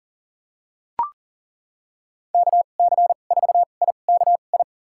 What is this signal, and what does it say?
Morse code, heard as a keyed tone, about 37 wpm, spaced as for 28 wpm, 700 Hz. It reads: KC4IKI
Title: USA Call Signs - 6 Characters - 2x Character Spacing - 40wpm